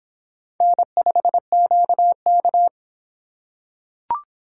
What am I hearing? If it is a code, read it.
Morse code, heard as a keyed tone, about 24 wpm, 700 Hz. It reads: N5QK